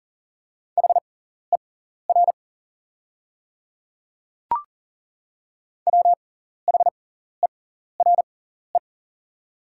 Morse code, heard as a keyed tone, about 40 wpm, spaced as for 14 wpm, 700 Hz. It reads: HER WHERE
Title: 6x Character Spacing - Top 300 Words - 40wpm